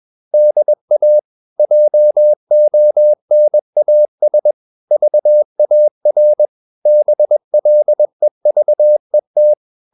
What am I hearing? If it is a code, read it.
Morse code, heard as a keyed tone, about 21 wpm, 600 Hz. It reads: DA JONAS VAR BLEVET
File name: Ferie QRS 21wpm